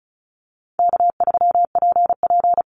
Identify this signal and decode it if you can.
Morse code, heard as a keyed tone, about 34 wpm, 700 Hz. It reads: K3PP
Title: Easy Call Signs Repeated 2x - 35wpm